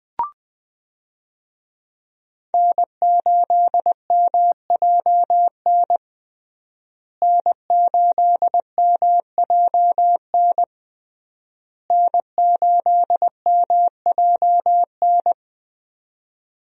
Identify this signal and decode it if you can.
Morse code, heard as a keyed tone, about 20 wpm, 700 Hz. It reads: N8MJN N8MJN N8MJN
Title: USA Call Signs - 5 Characters - Repeated 3x - 20wpm